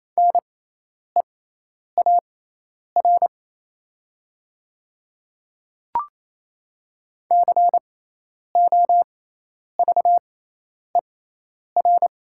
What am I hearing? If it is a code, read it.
Morse code, heard as a keyed tone, about 28 wpm, spaced as for 10 wpm, 690 Hz. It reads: NEAR COVER
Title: 6x Character Spacing - Top 300 Words - 28wpm